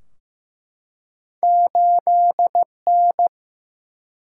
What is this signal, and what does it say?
Morse code, heard as a keyed tone, about 15 wpm, 700 Hz. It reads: 8N
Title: Single-Letter Single-Number 15wpm